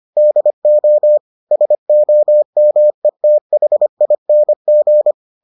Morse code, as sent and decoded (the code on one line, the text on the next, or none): -.. --- / ... --- -- . - .... .. -. --.
DO SOMETHING